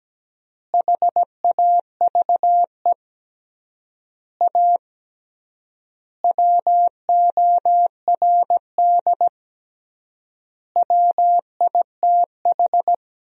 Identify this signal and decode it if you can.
Morse code, heard as a keyed tone, about 17 wpm, 700 Hz. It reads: HAVE A WORD WITH
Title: 3x Word-Spacing - Sentences from Top 300 Words - 17wpm